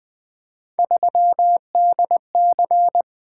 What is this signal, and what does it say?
Morse code, heard as a keyed tone, about 20 wpm, 700 Hz. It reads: 3DC
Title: Three Letter-Number Combinations - 20wpm